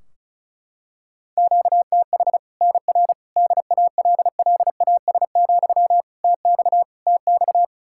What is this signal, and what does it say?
Morse code, heard as a keyed tone, about 36 wpm, 700 Hz. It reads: QTH NR DALLAS, TX TX
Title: Practice QSO - Exchange 1 - Beginning - 35wpm